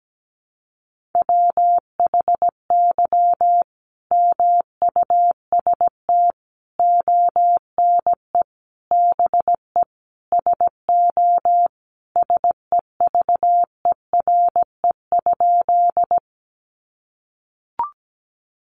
Morse code, heard as a keyed tone, about 17 wpm, 700 Hz. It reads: WHY MUST ONE BE SO SEVERE?
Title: Sentences from Top 1400 Words 17wpm